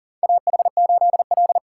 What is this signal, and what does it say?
Morse code, heard as a keyed tone, about 40 wpm, 700 Hz. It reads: AH8L
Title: USA Call Signs - 4 Characters - 3x Character Spacing - 40wpm